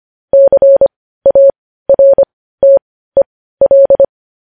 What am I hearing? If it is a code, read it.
Morse code, heard as a keyed tone, about 25 wpm, spaced as for 15 wpm, 550 Hz. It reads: CARTEL
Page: news headlines in morse code at 15 wpm